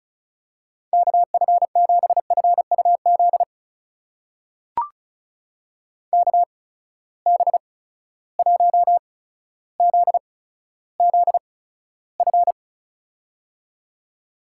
Morse code, heard as a keyed tone, about 36 wpm, spaced as for 15 wpm, 700 Hz. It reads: KF7FUZ KB1ZZF
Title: USA Call Signs - 6 Characters - 8x Character Spacing - 35wpm